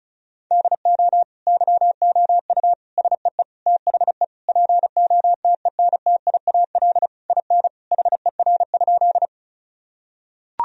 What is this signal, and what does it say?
Morse code, heard as a keyed tone, about 34 wpm, 700 Hz. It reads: DO YOU SEE THE POTENTIAL IN HER?
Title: Sentences from Top 1000 Words - Review Part 4 - 35wpm